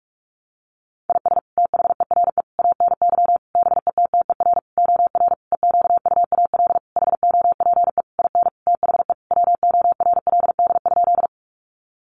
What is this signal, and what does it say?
Morse code, heard as a keyed tone, about 44 wpm, 700 Hz. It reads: IS THERE ANY BETTER OR EQUAL HOPE IN THE WORLD?
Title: Sentences from Top 1400 Words 45wpm